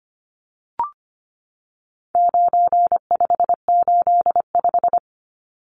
Morse code, heard as a keyed tone, about 25 wpm, 700 Hz. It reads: 9585
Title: Numbers - 4-Digit 25wpm